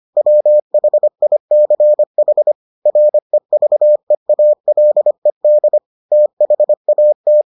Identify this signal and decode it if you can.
Morse code, heard as a keyed tone, about 25 wpm, 600 Hz. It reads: WHICH REVEALED THAT